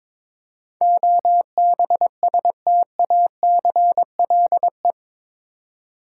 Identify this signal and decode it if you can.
Morse code, heard as a keyed tone, about 22 wpm, 700 Hz. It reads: OBSTACLE
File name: practice / 22wpm / Eight Letter Words - 22wpm